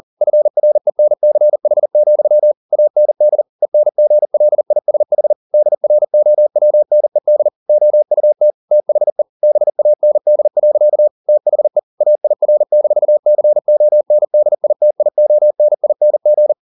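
Morse code, heard as a keyed tone, about 40 wpm, 600 Hz. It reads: FRENCH, AND ENGLISH DROWNED OUT THE BAND. THE AIR-CONDITIONING
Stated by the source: Morse code at 40 wpm, practice text, CW med Gnister.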